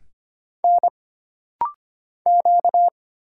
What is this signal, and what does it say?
Morse code, heard as a keyed tone, about 25 wpm, 700 Hz. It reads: N Q
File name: Single Character (Letter:Number:Punctuation:Prosign) - Rapid-Fire - 25wpm